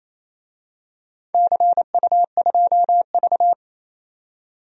Morse code, heard as a keyed tone, about 28 wpm, 700 Hz. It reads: CU2V
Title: DXCC-100 Call Signs - 4 Characters - 28wpm